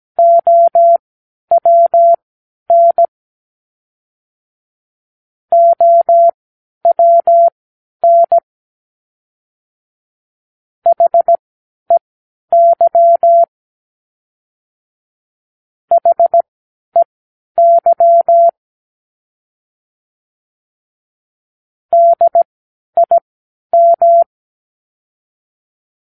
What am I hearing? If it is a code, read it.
Morse code, heard as a keyed tone, about 17 wpm, spaced as for 10 wpm, 680 Hz. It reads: OWN OWN HEY HEY DIM
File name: k7qo / Practicewords